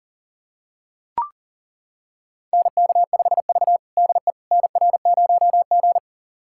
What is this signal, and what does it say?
Morse code, heard as a keyed tone, about 40 wpm, 700 Hz. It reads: NK5V DE NR0G
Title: USA Callsign DE Callsign - 4 Characters - 40wpm